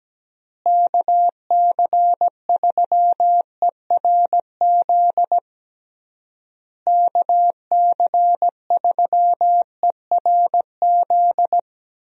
Morse code, heard as a keyed tone, about 17 wpm, 700 Hz. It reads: KC3ERZ KC3ERZ
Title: USA Call Signs - 6 Characters - Repeated 3x - 17wpm